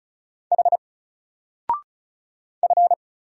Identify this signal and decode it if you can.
Morse code, heard as a keyed tone, about 36 wpm, 700 Hz. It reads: H F
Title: Single Letter-Number Alt 35wpm